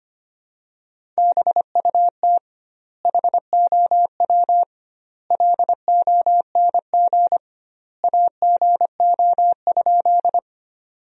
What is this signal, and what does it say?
Morse code, heard as a keyed tone, about 25 wpm, 700 Hz. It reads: BUT HOW LONG AGO?
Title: 2x Word-Spacing - Sentences from Top 500 Words - 25wpm